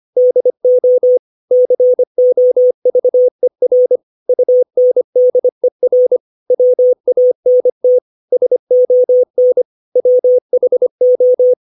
Morse code, heard as a keyed tone, about 25 wpm, 500 Hz. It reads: DO COVER UNDER WANT SON WHO
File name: 632 words / 632 Words 25WPM